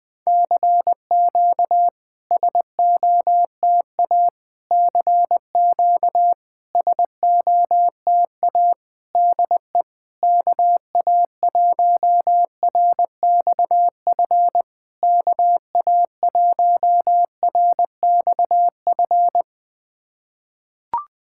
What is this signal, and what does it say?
Morse code, heard as a keyed tone, about 21 wpm, 690 Hz. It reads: CQ SOTA CQ SOTA DE KA1RXF KA1RXF
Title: SOTA QSO Practice 20wpm